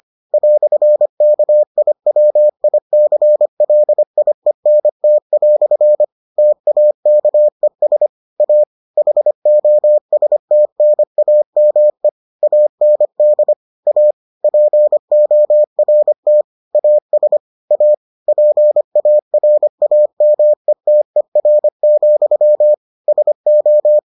Morse code, heard as a keyed tone, about 25 wpm, 600 Hz. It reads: "KIWICLIENT" TAKES A HOSTNAME AND A PORT AS A PARAMETER, SO